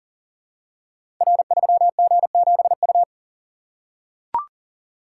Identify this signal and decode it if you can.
Morse code, heard as a keyed tone, about 38 wpm, 700 Hz. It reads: R3G7U
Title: Code Groups - 5 Letter:Numbers - 40wpm